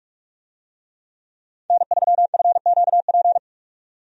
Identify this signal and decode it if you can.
Morse code, heard as a keyed tone, about 46 wpm, 700 Hz. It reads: N3FQP
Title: USA Call Signs - 5 Characters - 45wpm